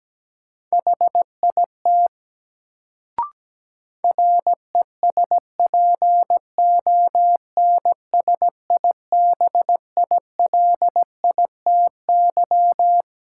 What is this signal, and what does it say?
Morse code, heard as a keyed tone, about 17 wpm, 700 Hz. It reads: HIT RESPONSIBILITY
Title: Sentences from Top 1200 Words - New Word Drill Alt - 17wpm